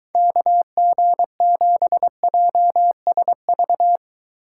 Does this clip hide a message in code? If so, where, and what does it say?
Yes: Morse code from 0.2 to 4.0 s, heard as a keyed tone, about 23 wpm, 700 Hz. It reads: KG7JSV